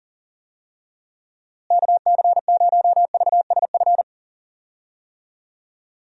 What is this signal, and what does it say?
Morse code, heard as a keyed tone, about 37 wpm, 700 Hz. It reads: KC0VSF
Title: USA Call Signs - 6 Characters - Repeated 2x - 40wpm